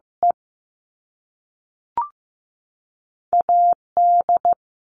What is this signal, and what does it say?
Morse code, heard as a keyed tone, about 15 wpm, 700 Hz. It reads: E AD
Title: All Cut Numbers - 2-Digit 15wpm